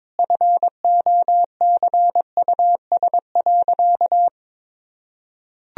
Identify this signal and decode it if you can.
Morse code, heard as a keyed tone, about 22 wpm, 700 Hz. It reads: FOCUS.
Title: Sentences from Top 1000 Words - Review Part 4 - 22wpm